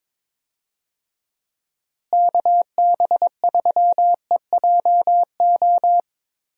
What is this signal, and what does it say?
Morse code, heard as a keyed tone, about 22 wpm, 700 Hz. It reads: KB3EJO
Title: USA Call Signs - 6 Characters - Repeated 3x - 22wpm